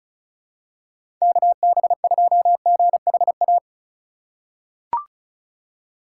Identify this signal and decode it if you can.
Morse code, heard as a keyed tone, about 36 wpm, 700 Hz. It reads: KB2GHA
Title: USA Call Signs - 6 Characters - 3x Character Spacing - 35wpm